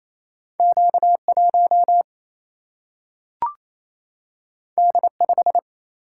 Morse code, heard as a keyed tone, about 28 wpm, 700 Hz. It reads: Q 1 D 5
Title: Single-Letter Single-Number 28wpm